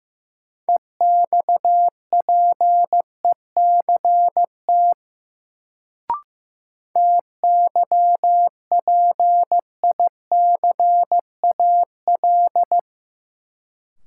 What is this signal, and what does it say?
Morse code, heard as a keyed tone, about 15 wpm, 700 Hz. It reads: EXPECT TYPICAL
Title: Sentences from Top 1100 Words - New Word Drill Alt - 15wpm